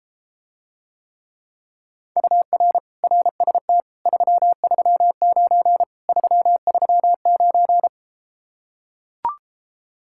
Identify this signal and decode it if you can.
Morse code, heard as a keyed tone, about 33 wpm, 700 Hz. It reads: UR RST 339 339